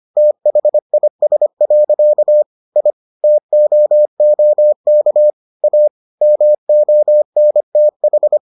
Morse code, heard as a keyed tone, about 25 wpm, 600 Hz. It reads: THIS. I TOOK A MONTH